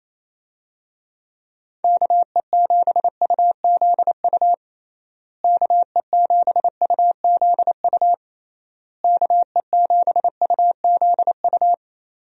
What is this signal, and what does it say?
Morse code, heard as a keyed tone, about 28 wpm, 700 Hz. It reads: KE7UZU KE7UZU KE7UZU
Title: USA Call Signs - 6 Characters - Repeated 3x - 28wpm